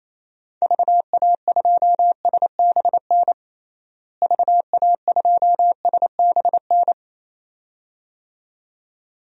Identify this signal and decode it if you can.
Morse code, heard as a keyed tone, about 27 wpm, 700 Hz. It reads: VA2SBN VA2SBN
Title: DXCC-100 Call Signs - 6 Characters - Repeated 2x - 28wpm